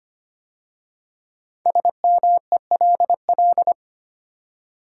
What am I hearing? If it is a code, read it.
Morse code, heard as a keyed tone, about 25 wpm, 700 Hz. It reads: SMELL
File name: 5 Letter Words - First 3 Letters Given - 25wpm